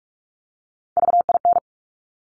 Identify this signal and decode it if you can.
Morse code, heard as a keyed tone, about 46 wpm, 700 Hz. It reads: VIN
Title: Top Three Letter Combinations in Words 45wpm